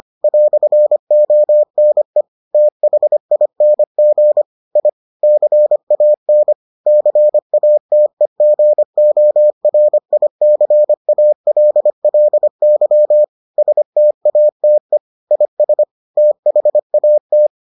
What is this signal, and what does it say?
Morse code, heard as a keyed tone, about 25 wpm, 600 Hz. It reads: "ONE THING I CAN CATEGORICALLY STATE IS THAT